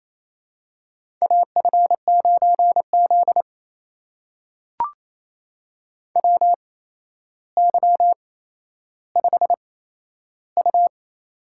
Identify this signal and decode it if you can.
Morse code, heard as a keyed tone, about 28 wpm, spaced as for 12 wpm, 700 Hz. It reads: AF9Z WY5U
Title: USA Call Signs - 4 Characters - 8x Character Spacing - 28wpm